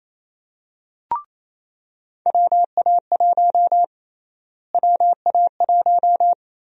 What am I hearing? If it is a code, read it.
Morse code, heard as a keyed tone, about 28 wpm, 700 Hz. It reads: WA1 WA1
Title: USA Call Sign Prefixes - Repeated 2x - 28wpm